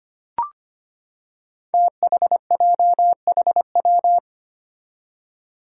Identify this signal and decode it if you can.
Morse code, heard as a keyed tone, about 25 wpm, 700 Hz. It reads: THJHW
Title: Five Challenging Letter-Number Combinations 25wpm